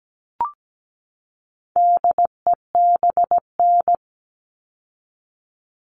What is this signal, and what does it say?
Morse code, heard as a keyed tone, about 17 wpm, 700 Hz. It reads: DEBN
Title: All Cut Numbers - 4-Digit 17wpm